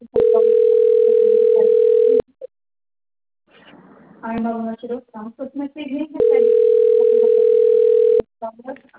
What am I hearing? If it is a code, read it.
Ringback tone (North America), 440 + 480 Hz, with speech.
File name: rg-1600-unknown- - - .73097